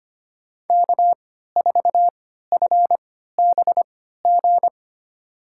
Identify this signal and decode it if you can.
Morse code, heard as a keyed tone, about 25 wpm, spaced as for 14 wpm, 700 Hz. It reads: K4FBG
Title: USA Call Signs - 5 Characters - 3x Character Spacing - 25wpm